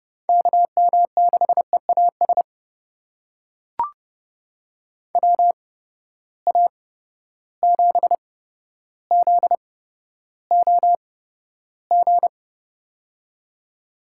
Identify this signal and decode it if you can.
Morse code, heard as a keyed tone, about 30 wpm, spaced as for 13 wpm, 700 Hz. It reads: KM6EAS WA7ZOG
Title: USA Call Signs - 6 Characters - 8x Character Spacing - 30wpm